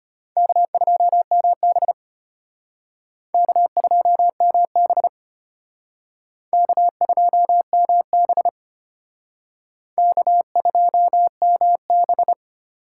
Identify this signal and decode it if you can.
Morse code, heard as a keyed tone, about 32 wpm, 700 Hz. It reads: K2MB K2MB K2MB K2MB